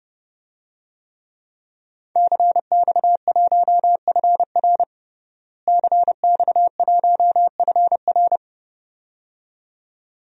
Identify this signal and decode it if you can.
Morse code, heard as a keyed tone, about 32 wpm, 700 Hz. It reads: CX1FR CX1FR
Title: DXCC-100 Call Signs - 5 Characters - Repeated 2x - 30wpm